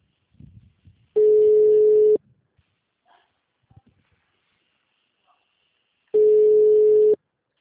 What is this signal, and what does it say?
Ringback tone (Europe, 425 Hz).
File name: out- -2005- - - .32